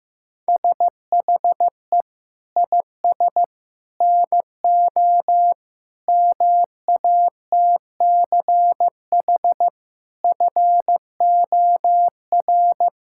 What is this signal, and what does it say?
Morse code, heard as a keyed tone, about 15 wpm, 700 Hz. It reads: SHE IS NO MATCH FOR
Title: Sentences from Top 1600 Words 15wpm